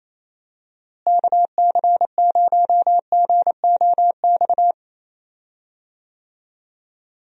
Morse code, heard as a keyed tone, about 28 wpm, 700 Hz. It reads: KC0GOX
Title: USA Call Signs - 6 Characters - Repeated 3x - 28wpm